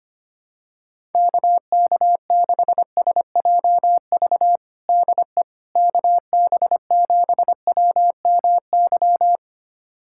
Morse code, heard as a keyed tone, about 25 wpm, 700 Hz. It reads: KK6SJV DE KB7WMY
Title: USA Callsign DE Callsign - 6 Characters - 25wpm